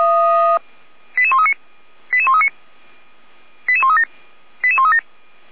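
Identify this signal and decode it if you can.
Five-tone selective call (ZVEI), heard as quick runs of five tones, tones of about 70 ms. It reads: 88139 88139 88137 88137
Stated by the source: ZVEI five-tone call 88335 received by SDR at Feuerwehr Orlamünde (f85360).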